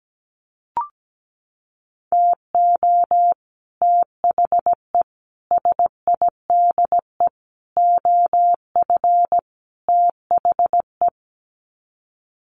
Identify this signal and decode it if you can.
Morse code, heard as a keyed tone, about 17 wpm, 700 Hz. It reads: TO THE SIDE OF THE
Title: Sets of 5 Words 17wpm